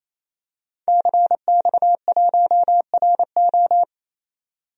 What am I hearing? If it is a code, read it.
Morse code, heard as a keyed tone, about 27 wpm, 700 Hz. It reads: CX1RO
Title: DXCC-100 Call Signs - 5 Characters - 28wpm